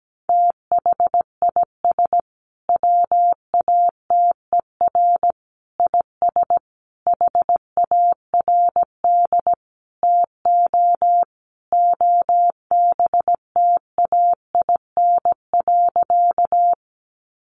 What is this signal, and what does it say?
Morse code, heard as a keyed tone, about 17 wpm, 700 Hz. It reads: THIS WATER IS HARD TO OBTAIN.
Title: Sentences from Top 1300 Words 17wpm